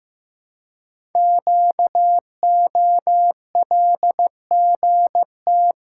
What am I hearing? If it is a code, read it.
Morse code, heard as a keyed tone, about 15 wpm, 700 Hz. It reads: QOLGT
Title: Code Groups - 5 Letters - 15wpm